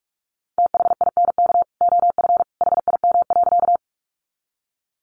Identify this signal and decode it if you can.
Morse code, heard as a keyed tone, about 44 wpm, 700 Hz. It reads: THINK OF HIM.
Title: Sentences from Top 200 Words 45wpm